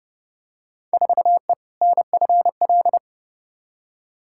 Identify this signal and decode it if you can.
Morse code, heard as a keyed tone, about 30 wpm, 700 Hz. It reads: 4E NFL
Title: Field Day Exchanges Encore - 30wpm